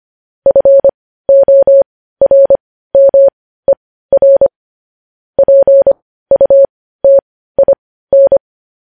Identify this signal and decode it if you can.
Morse code, heard as a keyed tone, about 25 wpm, spaced as for 15 wpm, 550 Hz. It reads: FORMER PUTIN
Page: news-headlines-in-morse-code-at-15-wpm